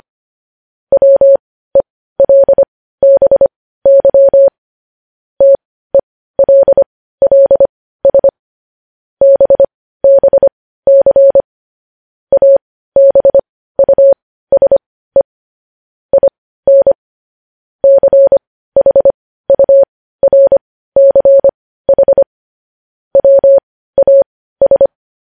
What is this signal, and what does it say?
Morse code, heard as a keyed tone, about 25 wpm, spaced as for 15 wpm, 550 Hz. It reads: WELBY TELLS BBC ABUSE IN CHURCH WAS
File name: podcast / News headlines-1x-25x15